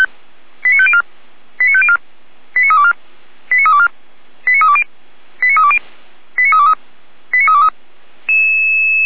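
Five-tone selective call (ZVEI), heard as quick runs of five tones, tones of about 70 ms. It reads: 89504 89504 89324 89324 89320 89320 89323 89323
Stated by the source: ZVEI five-tone call received by SDR, from feuerwehr-orlamuende.org (fire brigade Orlamünde) archive.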